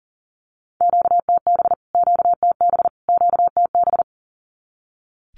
Morse code, heard as a keyed tone, about 40 wpm, 700 Hz. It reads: QTB QTB QTB
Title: Q-Codes Repeated 3x 40wpm